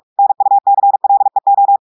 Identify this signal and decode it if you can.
Morse code, heard as a keyed tone, about 46 wpm, 800 Hz. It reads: NUCLEO
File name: audios / portugais / portugais texte en clair 1 vitesse 45 mots minutes